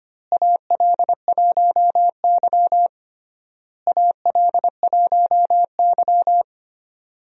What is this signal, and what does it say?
Morse code, heard as a keyed tone, about 25 wpm, 700 Hz. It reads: AL1Y AL1Y
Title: USA Call Signs - 4 Characters - Repeated 3x - 25wpm